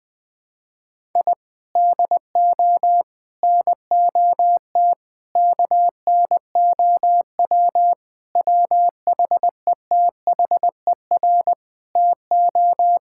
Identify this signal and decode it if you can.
Morse code, heard as a keyed tone, about 20 wpm, 700 Hz. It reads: I DO NOT KNOW WHETHER TO
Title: Sentences with 5-Word Phrases - 20wpm